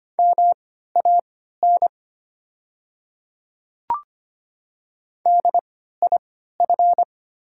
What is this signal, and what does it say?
Morse code, heard as a keyed tone, about 25 wpm, spaced as for 14 wpm, 700 Hz. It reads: MAN DIF
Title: 3x Character Spacing - Top 200 Words - 25wpm